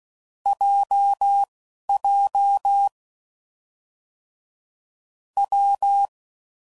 Morse code, heard as a keyed tone, about 16 wpm, spaced as for 11 wpm, 800 Hz. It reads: JJ W
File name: Combinations / Difficult Combos / WPJ WPJ1